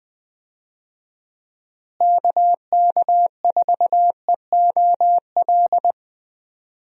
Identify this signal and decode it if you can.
Morse code, heard as a keyed tone, about 20 wpm, 700 Hz. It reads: KK4EOL
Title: USA Call Signs - 6 Characters - Repeated 3x - 20wpm